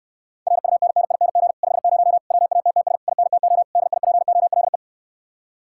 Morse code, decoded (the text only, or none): RUNNING 50 WATTS INTO DIPOLE